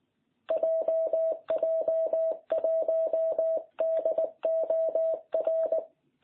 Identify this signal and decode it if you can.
Morse code, heard as a keyed tone, about 19 wpm, 660 Hz. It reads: JJ1DOR